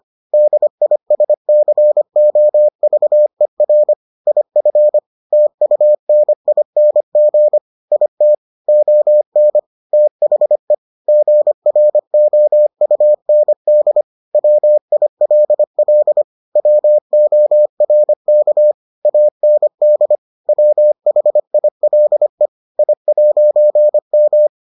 Morse code, heard as a keyed tone, about 25 wpm, 600 Hz. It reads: DISCOVER IF TUNING IT ON THE GROUND WILL WORK AND WHILE I'M